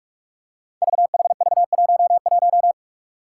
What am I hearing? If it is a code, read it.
Morse code, heard as a keyed tone, about 44 wpm, 700 Hz. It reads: VHV11
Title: Five Challenging Letter-Number Combinations 45wpm